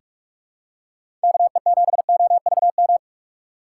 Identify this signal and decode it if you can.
Morse code, heard as a keyed tone, about 42 wpm, 700 Hz. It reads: KE7OVM
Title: USA Call Signs - 6 Characters - 3x Character Spacing - 45wpm